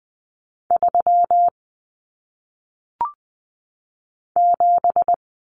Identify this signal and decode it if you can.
Morse code, heard as a keyed tone, about 20 wpm, 700 Hz. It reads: 3 7
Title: Numbers - 1-Digit 20wpm